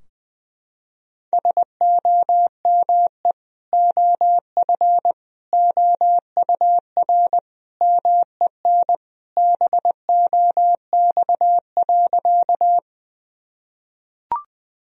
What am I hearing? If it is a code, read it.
Morse code, heard as a keyed tone, about 20 wpm, 700 Hz. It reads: SOME OF OUR MEN BOX.
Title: Sentences from Top 1000 Words - Review Part 2 - 20wpm